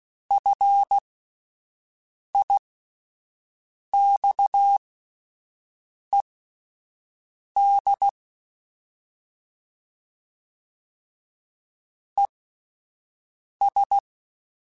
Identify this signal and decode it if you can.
Morse code, heard as a keyed tone, about 16 wpm, spaced as for 5 wpm, 800 Hz. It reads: FIXED ES
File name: (spaced) / QSOs Spaced-mp3s / WA7QOX 1 @10 sp